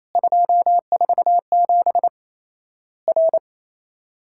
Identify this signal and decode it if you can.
Morse code, heard as a keyed tone, about 28 wpm, 700 Hz. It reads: 247 R
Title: Worked All Europe DX Contest - QTC Exchange - 28wpm